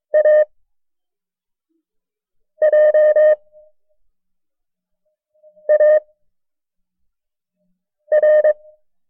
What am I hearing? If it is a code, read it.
Morse code, heard as a keyed tone, about 23 wpm, spaced as for 4 wpm, 610 Hz. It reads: AJAR
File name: FreeFiles / CW-practice / KB6MT-tape1-side2